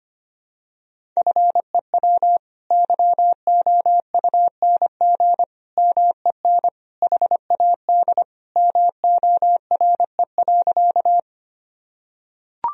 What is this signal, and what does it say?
Morse code, heard as a keyed tone, about 25 wpm, 700 Hz. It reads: FEW YOUNG MEN HAD MORE.
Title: Sentences from Top 1000 Words - Review Part 1 - 25wpm